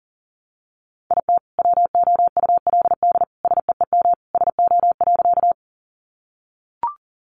Morse code, heard as a keyed tone, about 40 wpm, 700 Hz. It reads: IT WOULD SEEM SO.
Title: Sentences from Top 300 Words 40wpm